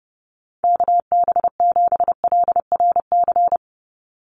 Morse code, heard as a keyed tone, about 30 wpm, 700 Hz. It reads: KB7LRC
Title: USA Call Signs 30wpm